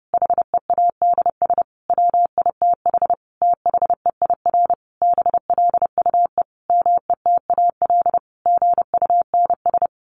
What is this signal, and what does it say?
Morse code, heard as a keyed tone, about 30 wpm, 700 Hz. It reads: HEADS WITH THEIR BLUE METAL GUNS
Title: A Sound Of Thunder 30wpm